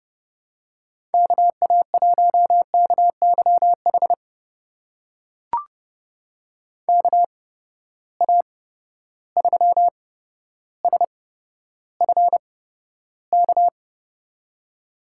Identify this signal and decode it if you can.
Morse code, heard as a keyed tone, about 30 wpm, spaced as for 13 wpm, 700 Hz. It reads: KA1KYH KA3SFK
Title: USA Call Signs - 8x Character Spacing - 30wpm